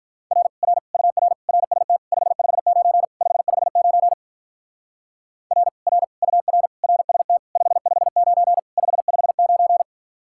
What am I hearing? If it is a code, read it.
Morse code, heard as a keyed tone, about 46 wpm, 700 Hz. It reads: R R UR RST 559 559 R R UR RST 559 559